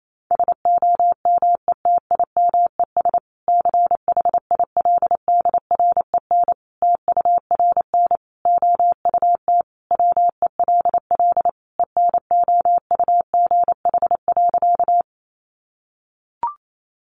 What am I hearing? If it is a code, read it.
Morse code, heard as a keyed tone, about 28 wpm, 700 Hz. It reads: SOMETIMES CHILDREN TURN OUT WELL ENOUGH.
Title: Sentences from Top 600 Words 28wpm